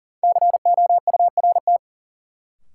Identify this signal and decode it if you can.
Morse code, heard as a keyed tone, about 40 wpm, 700 Hz. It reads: COURT
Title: Sentences from Top 1600 Words - New Word Drill Alt - 40wpm